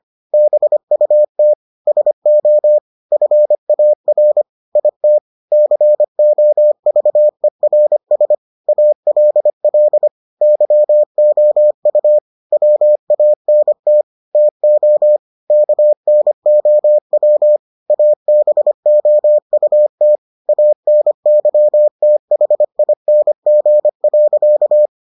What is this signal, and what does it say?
Morse code, heard as a keyed tone, about 25 wpm, 600 Hz. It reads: BUT SO FAR IT COVERS ALL YOU WANT TO KNOW ABOUT ANYTHING.